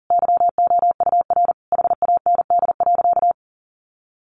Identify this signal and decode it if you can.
Morse code, heard as a keyed tone, about 40 wpm, 700 Hz. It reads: YOUR HAND.
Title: Sentences from Top 400 Words 40wpm